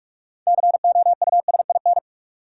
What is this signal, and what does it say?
Morse code, heard as a keyed tone, about 44 wpm, 700 Hz. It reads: COUSIN
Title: Six Letter Words - 45wpm